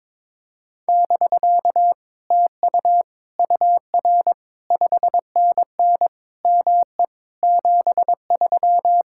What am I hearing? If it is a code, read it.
Morse code, heard as a keyed tone, about 22 wpm, 700 Hz. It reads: <BK> TU UR 5NN ME 73
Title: Speed-Racing Common QSO Phrases - 22wpm